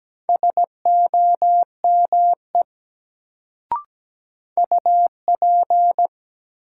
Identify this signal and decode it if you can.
Morse code, heard as a keyed tone, about 17 wpm, 700 Hz. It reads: SOME UP
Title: Sets of 2 Words - New Word Drill - 17wpm